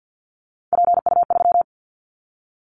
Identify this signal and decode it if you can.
Morse code, heard as a keyed tone, about 50 wpm, 700 Hz. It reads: PU3
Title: Callsign Prefixes for Top 100 DXCC - 50wpm